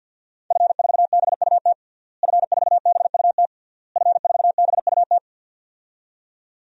Morse code, heard as a keyed tone, about 46 wpm, 700 Hz. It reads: F4BUT F4BUT F4BUT
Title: DXCC-100 Call Signs - 5 Characters - Repeated 3x - 50wpm